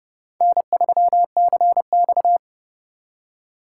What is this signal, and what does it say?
Morse code, heard as a keyed tone, about 39 wpm, 700 Hz. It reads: N3CX